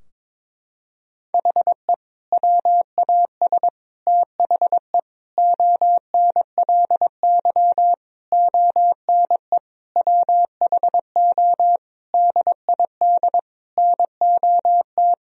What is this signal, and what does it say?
Morse code, heard as a keyed tone, about 22 wpm, 700 Hz. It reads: HE WAS THE ONLY ONE WHO DID NOT
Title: Sentences from Top 1000 Words - Review Part 3 - 22wpm